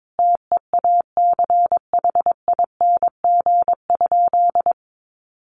Morse code, heard as a keyed tone, about 22 wpm, 700 Hz. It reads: TEACHING?
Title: Sentences from Top 1100 Words 22wpm